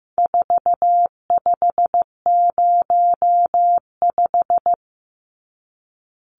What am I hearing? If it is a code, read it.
Morse code, heard as a keyed tone, about 15 wpm, 700 Hz. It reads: 4505
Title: Numbers - 4-Digit 15wpm